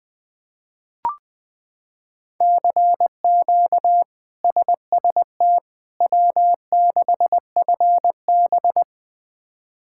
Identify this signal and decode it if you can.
Morse code, heard as a keyed tone, about 20 wpm, 700 Hz. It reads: CQ SST W6FB
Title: K1USN SST QSO Practice - 20wpm